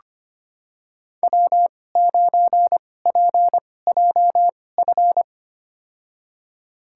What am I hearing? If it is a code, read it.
Morse code, heard as a keyed tone, about 25 wpm, spaced as for 18 wpm, 700 Hz. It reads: W9PJF
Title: USA Call Signs - 5 Characters - 2x Character Spacing - 25wpm